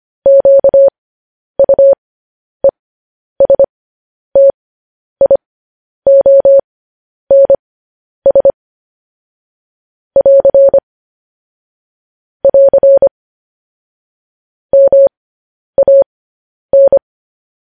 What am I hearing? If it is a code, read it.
Morse code, heard as a keyed tone, about 25 wpm, spaced as for 10 wpm, 550 Hz. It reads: QUESTIONS <AR> <AR> MAN
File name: podcast / News headlines-1x-25x10